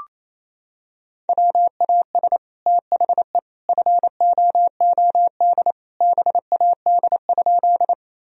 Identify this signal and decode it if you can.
Morse code, heard as a keyed tone, about 27 wpm, 700 Hz. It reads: WAS THE FOOD BAD?
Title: Sentences from Top 1000 Words - Review Part 1 - 28wpm